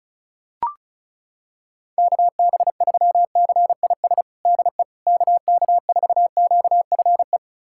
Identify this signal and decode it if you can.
Morse code, heard as a keyed tone, about 34 wpm, 700 Hz. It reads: KB3CIS DE KK4QFE
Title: USA Callsign DE Callsign - 6 Characters - 35wpm